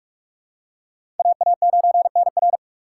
Morse code, heard as a keyed tone, about 45 wpm, 700 Hz. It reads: AA9NR
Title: USA Call Signs - 5 Characters - 4x Character Spacing - 45wpm